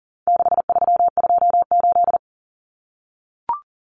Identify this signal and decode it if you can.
Morse code, heard as a keyed tone, about 40 wpm, 700 Hz. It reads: 6328
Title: Common Cut Numbers - 4-Digit 40wpm